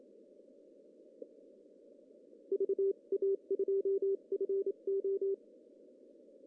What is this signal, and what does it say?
Morse code, heard as a keyed tone, about 27 wpm, spaced as for 24 wpm, 390 Hz. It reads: VA2FO
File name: CW / cqww-cw-k5tr